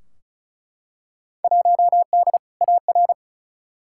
Morse code, heard as a keyed tone, about 36 wpm, 700 Hz. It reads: 1D AR
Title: Field Day Exchanges - 35wpm